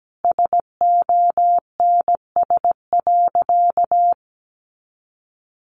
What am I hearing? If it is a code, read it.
Morse code, heard as a keyed tone, about 17 wpm, 700 Hz. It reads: SONS.
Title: A Sound Of Thunder 17wpm